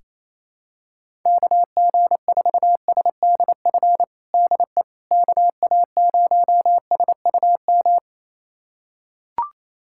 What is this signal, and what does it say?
Morse code, heard as a keyed tone, about 28 wpm, 700 Hz. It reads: KG4SDF DE KA0SUM
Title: USA Callsign DE Callsign - 6 Characters - 28wpm